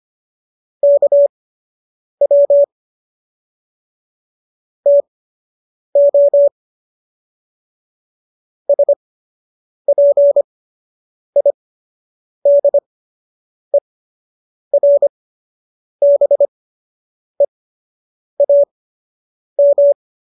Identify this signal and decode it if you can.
Morse code, heard as a keyed tone, about 25 wpm, spaced as for 8 wpm, 570 Hz. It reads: KW TO SPIDERBEAM